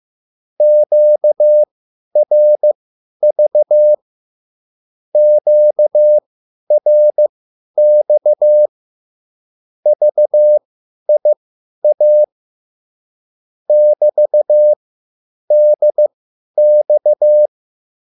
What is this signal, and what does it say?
Morse code, heard as a keyed tone, about 15 wpm, spaced as for 10 wpm, 600 Hz. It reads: QRV QRX VIA <BT>DX